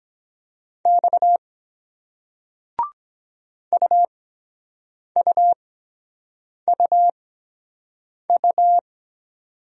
Morse code, heard as a keyed tone, about 23 wpm, spaced as for 7 wpm, 700 Hz. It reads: X UUUU